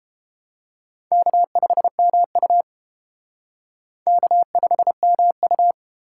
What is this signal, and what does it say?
Morse code, heard as a keyed tone, about 29 wpm, 700 Hz. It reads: K5MU K5MU